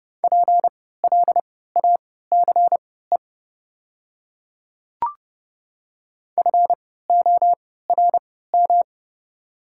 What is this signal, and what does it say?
Morse code, heard as a keyed tone, about 29 wpm, spaced as for 17 wpm, 700 Hz. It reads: PLACE FORM
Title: 3x Character Spacing - Top 200 Words - 30wpm